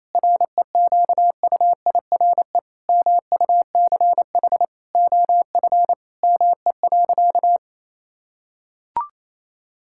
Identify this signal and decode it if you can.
Morse code, heard as a keyed tone, about 28 wpm, 700 Hz. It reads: REQUIRE MUCH OF ME.
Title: Sentences from Top 1000 Words - Review Part 6 - 28wpm